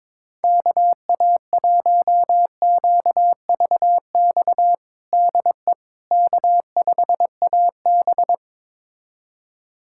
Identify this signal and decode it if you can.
Morse code, heard as a keyed tone, about 22 wpm, 700 Hz. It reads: KA1QVX DE K5AB
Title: USA Callsign DE Callsign 22wpm